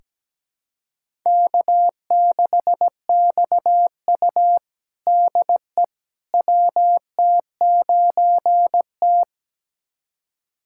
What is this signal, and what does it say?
Morse code, heard as a keyed tone, about 17 wpm, 700 Hz. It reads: K6XU DE WT9T
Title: USA Callsign DE Callsign - 4 Characters - 17wpm